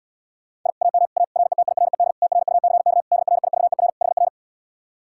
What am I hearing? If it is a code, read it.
Morse code, heard as a keyed tone, about 80 wpm, 700 Hz. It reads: I AM A RETIRED TRUCK DRIVER HR